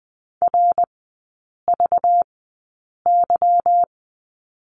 Morse code, heard as a keyed tone, about 20 wpm, spaced as for 8 wpm, 700 Hz. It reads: RVY
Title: 198 Lesson 32e - Letter k - Three Characters 20wpm